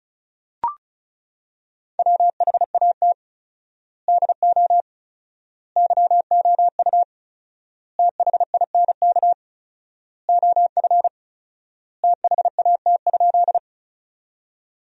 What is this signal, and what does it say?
Morse code, heard as a keyed tone, about 34 wpm, 700 Hz. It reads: WHAT DO YOU THINK OF THAT?
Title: 4x Word-Spacing - Sentences from Top 200 Words - 35wpm